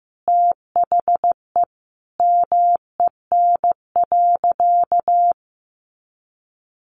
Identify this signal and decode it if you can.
Morse code, heard as a keyed tone, about 15 wpm, 700 Hz. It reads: THE MEN.
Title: Sentences from Top 1400 Words 15wpm